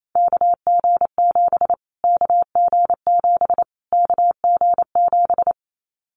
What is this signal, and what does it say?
Morse code, heard as a keyed tone, about 28 wpm, 700 Hz. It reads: KG7 KG7 KG7
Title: Callsign Prefixes for Top 25 DXCC - Repeated 3x - 28wpm